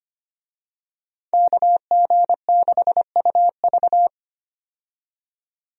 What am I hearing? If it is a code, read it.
Morse code, heard as a keyed tone, about 25 wpm, 700 Hz. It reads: KG6UV
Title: USA Call Signs - 5 Characters - 25wpm